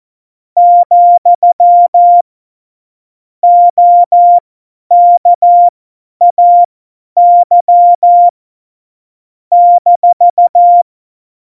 Morse code, heard as a keyed tone, about 14 wpm, spaced as for 10 wpm, 700 Hz. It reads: , OKAY -
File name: Morsekurs / Witze / Witz 19. Januar 2021, 50 F70